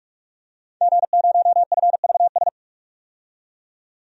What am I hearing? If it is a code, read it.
Morse code, heard as a keyed tone, about 45 wpm, 700 Hz. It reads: G0FVS
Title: DXCC-100 Call Signs - 5 Characters - 45wpm